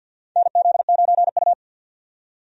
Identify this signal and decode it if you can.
Morse code, heard as a keyed tone, about 49 wpm, 700 Hz. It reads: NZ9U
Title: USA Call Signs - 4 Characters - 50wpm